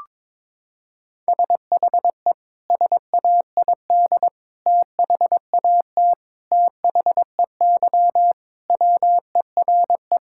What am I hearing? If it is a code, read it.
Morse code, heard as a keyed tone, about 22 wpm, 700 Hz. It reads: SHE SAID THAT THEY WERE
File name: practice / 22wpm / Sentences with 3-Word Phrases - 22wpm